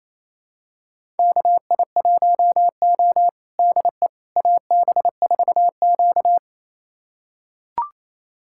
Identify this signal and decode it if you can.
Morse code, heard as a keyed tone, about 28 wpm, 700 Hz. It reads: KI1O DE AB4Q
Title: USA Callsign DE Callsign - 4 Characters - 28wpm